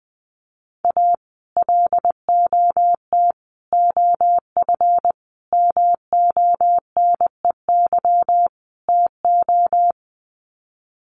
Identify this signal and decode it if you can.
Morse code, heard as a keyed tone, about 20 wpm, 700 Hz. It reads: A LOT OF MONEY TO
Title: Sets of 5 Words Encore2 20wpm